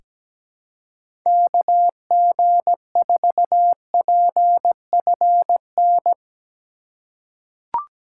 Morse code, heard as a keyed tone, about 17 wpm, 700 Hz. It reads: KG4PFN
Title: USA Call Signs - 6 Characters - 3x Character Spacing - 17wpm